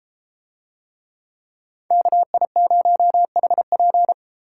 Morse code, heard as a keyed tone, about 31 wpm, 700 Hz. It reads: KI0HP